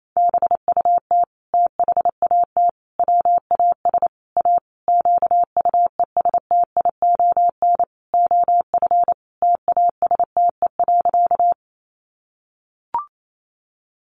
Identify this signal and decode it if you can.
Morse code, heard as a keyed tone, about 28 wpm, 700 Hz. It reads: BUT THAT WAS A QUESTION OF TASTE.
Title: Sentences from Top 1100 Words 28wpm